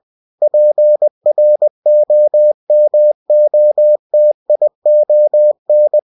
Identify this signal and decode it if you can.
Morse code, heard as a keyed tone, about 20 wpm, 600 Hz. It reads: PROMOTION
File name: cw-books / winnie-the-pooh / winnie-20wpm-page-0120